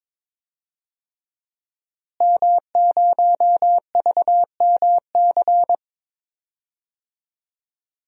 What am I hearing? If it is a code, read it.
Morse code, heard as a keyed tone, about 22 wpm, 700 Hz. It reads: M0VMC
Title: DXCC-100 Call Signs - 5 Characters - Repeated 2x - 22wpm